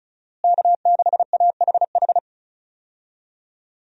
Morse code, heard as a keyed tone, about 34 wpm, 700 Hz. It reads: K6AHH
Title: USA Call Signs - 5 Characters - Repeated 3x - 35wpm